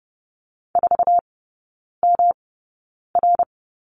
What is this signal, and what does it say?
Morse code, heard as a keyed tone, about 30 wpm, spaced as for 9 wpm, 700 Hz. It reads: 4MR
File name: Lesson 18e - Letter m - Three Characters 30wpm